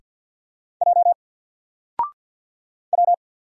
Morse code, heard as a keyed tone, about 50 wpm, 700 Hz. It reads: J W
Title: Single Letter-Number 50wpm